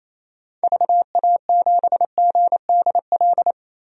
Morse code, heard as a keyed tone, about 27 wpm, 700 Hz. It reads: VA7GDL
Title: DXCC-100 Call Signs - 6 Characters - 28wpm